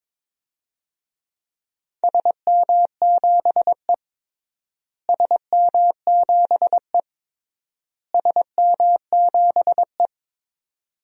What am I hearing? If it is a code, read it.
Morse code, heard as a keyed tone, about 21 wpm, 700 Hz. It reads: SM7E SM7E SM7E
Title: DXCC-100 Call Signs - 4 Characters - Repeated 3x - 22wpm